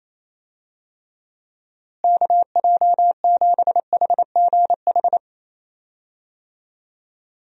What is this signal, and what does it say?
Morse code, heard as a keyed tone, about 28 wpm, 700 Hz. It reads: KJ7HGH
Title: USA Call Signs - 6 Characters - Repeated 2x - 28wpm